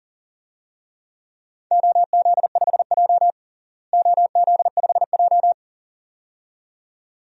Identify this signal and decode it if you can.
Morse code, heard as a keyed tone, about 37 wpm, 700 Hz. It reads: OZ5J OZ5J
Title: DXCC-100 Call Signs - 4 Characters - Repeated 2x - 40wpm